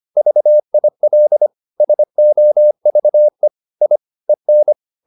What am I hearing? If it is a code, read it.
Morse code, heard as a keyed tone, about 25 wpm, 600 Hz. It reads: VIL SOVE I EN
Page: Ferie 25 wpm